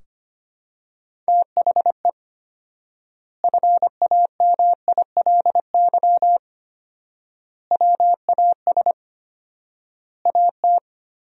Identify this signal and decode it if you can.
Morse code, heard as a keyed tone, about 25 wpm, 700 Hz. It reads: THE FAMILY WAS AT
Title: 4x Word-Spacing - Sentences from Top 400 Words - 25wpm